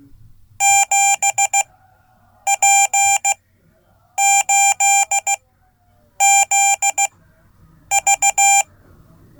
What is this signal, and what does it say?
Morse code, heard as a keyed tone, about 16 wpm, spaced as for 8 wpm, 790 Hz. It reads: 7P8ZV
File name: Code / 5min-b